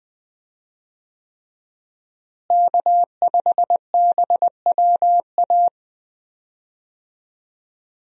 Morse code, heard as a keyed tone, about 20 wpm, 700 Hz. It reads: K5BWA
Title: USA Call Signs - 5 Characters - Repeated 3x - 20wpm